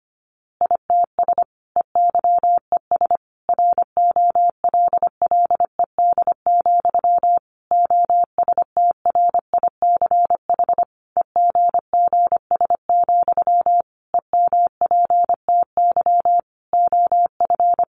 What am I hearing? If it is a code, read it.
Morse code, heard as a keyed tone, about 25 wpm, 700 Hz. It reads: ITS EYES ROLLED, OSTRICH EGGS, EMPTY OF
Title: A Sound Of Thunder 25wpm